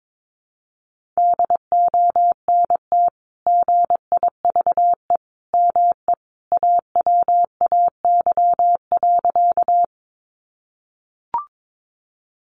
Morse code, heard as a keyed tone, about 22 wpm, 700 Hz. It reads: DONT GIVE ME AWAY.
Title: Sentences from Top 600 Words 22wpm